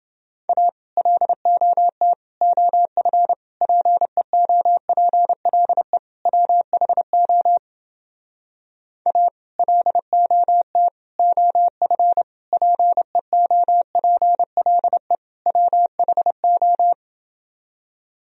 Morse code, heard as a keyed tone, about 30 wpm, 700 Hz. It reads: A LOT OF PEOPLE WHO A LOT OF PEOPLE WHO